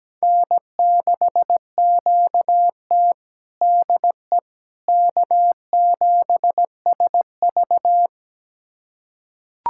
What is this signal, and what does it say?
Morse code, heard as a keyed tone, about 17 wpm, 700 Hz. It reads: N6QT DE K7SV
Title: USA Callsign DE Callsign - 4 Characters - 17wpm